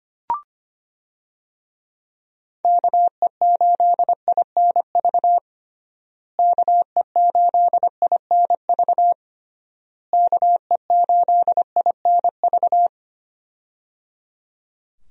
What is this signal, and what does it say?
Morse code, heard as a keyed tone, about 25 wpm, 700 Hz. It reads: KE8INV KE8INV KE8INV
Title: USA Call Signs - 6 Characters - Repeated 3x - 25wpm